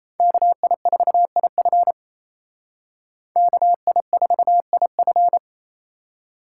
Morse code, heard as a keyed tone, about 35 wpm, 700 Hz. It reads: KI4IF KI4IF